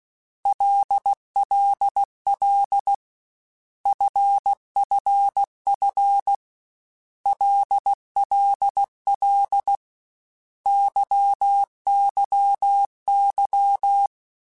Morse code, heard as a keyed tone, about 16 wpm, 800 Hz. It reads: LLL FFF LLL YYY
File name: Basic Letter Groups By7Groups / LFCYQ / LFCYQ@13